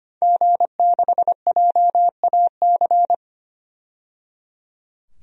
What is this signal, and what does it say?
Morse code, heard as a keyed tone, about 25 wpm, 700 Hz. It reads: G6JAC
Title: DXCC-100 Call Signs - 5 Characters - 25wpm